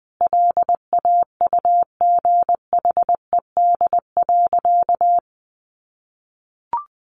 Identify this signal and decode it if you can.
Morse code, heard as a keyed tone, about 20 wpm, 700 Hz. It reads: LAUGHED.
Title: A Sound Of Thunder 20wpm